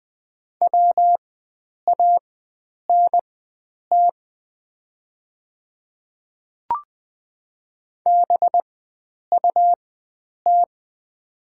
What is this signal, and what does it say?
Morse code, heard as a keyed tone, about 20 wpm, spaced as for 9 wpm, 700 Hz. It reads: WANT BUT
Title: 4x Character Spacing - Top 500 Words - 20wpm